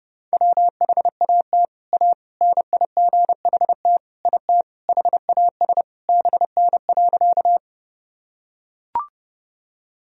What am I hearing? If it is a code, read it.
Morse code, heard as a keyed tone, about 32 wpm, 700 Hz. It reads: WHAT A NIGHT IT HAS BN.
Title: Sentences from Top 300 Words and Abbreviations 30wpm